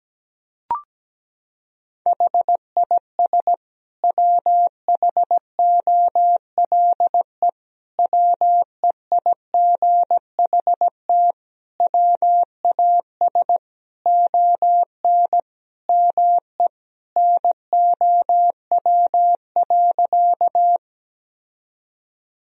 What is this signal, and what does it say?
Morse code, heard as a keyed tone, about 17 wpm, 700 Hz. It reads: HIS WHOLE WEIGHT WAS ON ME NOW.
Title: Sentences from Top 1000 Words - Review Part 3 - 17wpm